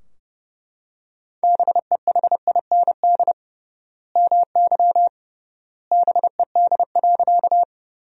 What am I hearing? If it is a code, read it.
Morse code, heard as a keyed tone, about 32 wpm, 700 Hz. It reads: BEHIND MY BED.
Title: 3x Word-Spacing - Sentences from Top 500 Words - 30wpm